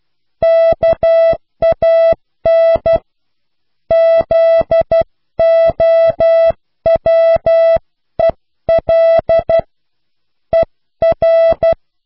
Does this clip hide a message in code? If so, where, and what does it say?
Yes: Morse code from 0.4 to 11.7 s, heard as a keyed tone, about 12 wpm, 660 Hz. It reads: KAN ZOWEL ER